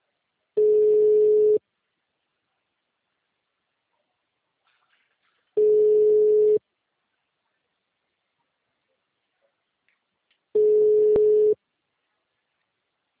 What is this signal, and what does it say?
Ringback tone (Europe, 425 Hz).